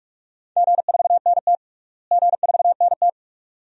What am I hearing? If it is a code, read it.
Morse code, heard as a keyed tone, about 47 wpm, 700 Hz. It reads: G4NT G4NT
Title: DXCC-100 Call Signs - 4 Characters - Repeated 2x - 45wpm